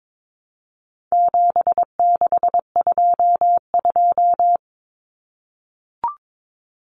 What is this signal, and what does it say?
Morse code, heard as a keyed tone, about 22 wpm, 700 Hz. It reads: 7622
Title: Common Cut Numbers - 4-Digit 22wpm